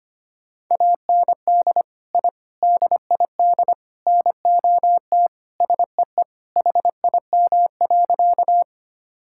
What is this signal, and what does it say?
Morse code, heard as a keyed tone, about 25 wpm, 700 Hz. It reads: AND I DID NOT SEE HIM.
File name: practice / 25wpm / Sentences with 4-Word Phrases - 25wpm